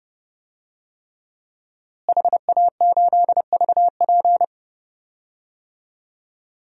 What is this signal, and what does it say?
Morse code, heard as a keyed tone, about 29 wpm, 700 Hz. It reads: HA8VP
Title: DXCC-100 Call Signs - 5 Characters - Repeated 3x - 30wpm